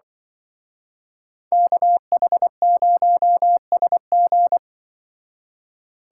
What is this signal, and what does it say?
Morse code, heard as a keyed tone, about 23 wpm, 700 Hz. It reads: KH0SG
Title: Speed Racing - USA 2x2 Call Signs 20wpm